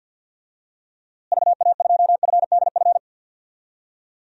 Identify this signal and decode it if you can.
Morse code, heard as a keyed tone, about 47 wpm, 700 Hz. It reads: VA2FDF
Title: DXCC-100 Call Signs - 6 Characters - Repeated 2x - 50wpm